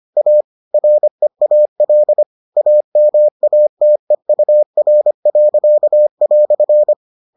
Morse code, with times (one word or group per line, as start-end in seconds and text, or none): 0.2-0.4 A
0.7-2.2 REAL
2.6-6.9 AMATEUR."